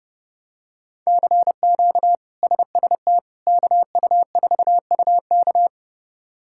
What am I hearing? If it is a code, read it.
Morse code, heard as a keyed tone, about 30 wpm, 700 Hz. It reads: CQ SST KU4UK
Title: K1USN SST QSO Practice - 30wpm